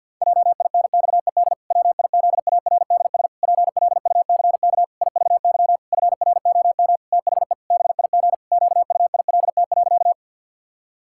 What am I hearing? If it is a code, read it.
Morse code, heard as a keyed tone, about 50 wpm, 700 Hz. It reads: JINXED WIZARDS PLUCK IVY FROM THE BIG QUILT.
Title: Pangram Challenge v2 - 50wpm